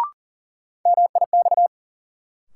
Morse code, heard as a keyed tone, about 40 wpm, 700 Hz. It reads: MIX
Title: Sentences from Top 1100 Words - New Word Drill Alt - 40wpm